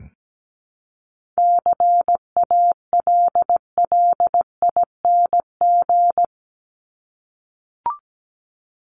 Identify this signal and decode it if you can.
Morse code, heard as a keyed tone, about 17 wpm, 700 Hz. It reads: CALLING
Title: ING Suffix - Top 100 Words with Base Words 17wpm